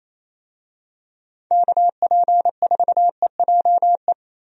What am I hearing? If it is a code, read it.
Morse code, heard as a keyed tone, about 28 wpm, 700 Hz. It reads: KP4EJE
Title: DXCC-100 Call Signs - 6 Characters - Repeated 3x - 28wpm